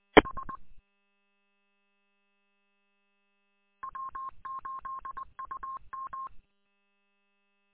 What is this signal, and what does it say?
Morse code, heard as a keyed tone, about 20 wpm, 1070 Hz. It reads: H W8UM
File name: w8um- -combined